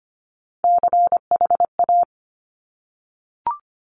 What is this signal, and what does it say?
Morse code, heard as a keyed tone, about 25 wpm, 700 Hz. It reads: CHA
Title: Top Three Letter Combinations in Words 25wpm